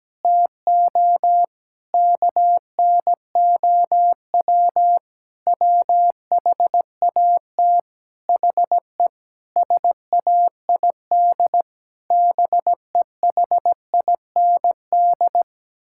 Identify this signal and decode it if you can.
Morse code, heard as a keyed tone, about 17 wpm, 700 Hz. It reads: TO KNOW WHAT HE SAID BEHIND
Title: Sentences from Top 1000 Words - Review Part 6 - 17wpm